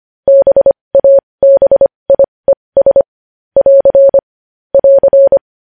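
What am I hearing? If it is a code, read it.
Morse code, heard as a keyed tone, about 25 wpm, spaced as for 20 wpm, 550 Hz. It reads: BABIES <AR> <AR>
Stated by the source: Morse code news headlines sent at 20 wpm.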